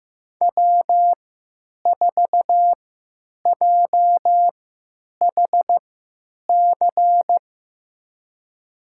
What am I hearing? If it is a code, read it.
Morse code, heard as a keyed tone, about 15 wpm, spaced as for 9 wpm, 700 Hz. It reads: W4JHC
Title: USA Call Signs - 5 Characters - 3x Character Spacing - 15wpm